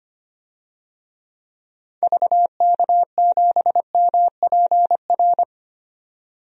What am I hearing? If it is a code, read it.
Morse code, heard as a keyed tone, about 25 wpm, 700 Hz. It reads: VK7MPR
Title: DXCC-100 Call Signs - 6 Characters - 25wpm